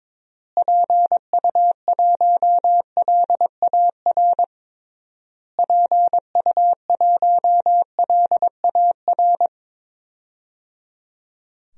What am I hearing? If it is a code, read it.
Morse code, heard as a keyed tone, about 22 wpm, 700 Hz. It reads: PU1LAR PU1LAR
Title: DXCC-100 Call Signs - 6 Characters - Repeated 3x - 22wpm